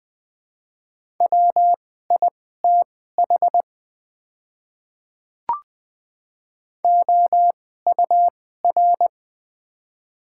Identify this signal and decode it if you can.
Morse code, heard as a keyed tone, about 20 wpm, spaced as for 14 wpm, 700 Hz. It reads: WITH OUR
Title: 2x Character Spacing - Top 200 Words - 20wpm